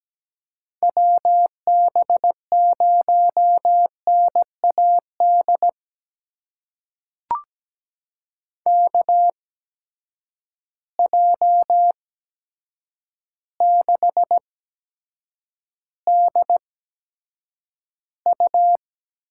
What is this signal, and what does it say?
Morse code, heard as a keyed tone, about 17 wpm, spaced as for 8 wpm, 700 Hz. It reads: WB0NAD KJ6DU
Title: USA Call Signs - 6 Characters - 8x Character Spacing - 17wpm